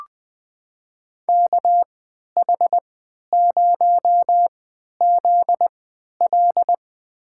Morse code, heard as a keyed tone, about 20 wpm, spaced as for 11 wpm, 700 Hz. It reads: KH0ZL
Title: USA Call Signs - 5 Characters - 3x Character Spacing - 20wpm